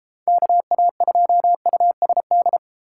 Morse code, heard as a keyed tone, about 30 wpm, 700 Hz. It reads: KA2USD